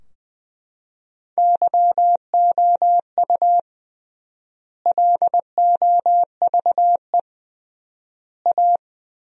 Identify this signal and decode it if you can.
Morse code, heard as a keyed tone, about 21 wpm, 700 Hz. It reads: YOU LOVE A
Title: 3x Word-Spacing - Sentences from Top 500 Words - 20wpm